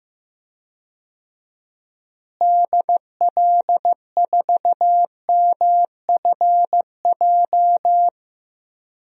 Morse code, heard as a keyed tone, about 15 wpm, 700 Hz. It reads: DL4MFJ
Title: DXCC-100 Call Signs - 6 Characters - Repeated 3x - 15wpm